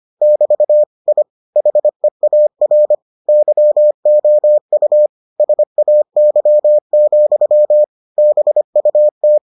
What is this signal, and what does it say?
Morse code, heard as a keyed tone, about 25 wpm, 600 Hz. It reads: <BT> I HEAR YOU SAY, BUT